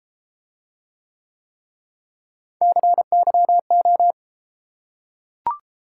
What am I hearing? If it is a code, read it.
Morse code, heard as a keyed tone, about 32 wpm, 700 Hz. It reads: CYO